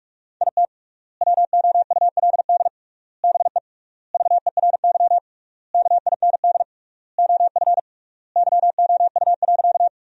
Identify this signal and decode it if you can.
Morse code, heard as a keyed tone, about 46 wpm, 700 Hz. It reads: IT WOULD BE VERY KIND OF YOU.
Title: 3x Word-Spacing - Sentences from Top 200 Words - 45wpm